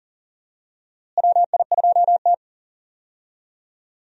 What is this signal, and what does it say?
Morse code, heard as a keyed tone, about 40 wpm, 700 Hz. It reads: WI2T
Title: USA Call Signs - 4 Characters - Repeated 2x - 40wpm